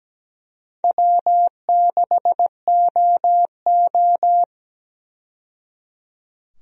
Morse code, heard as a keyed tone, about 17 wpm, 700 Hz. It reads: W6OO
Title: USA Call Signs - 4 Characters - 17wpm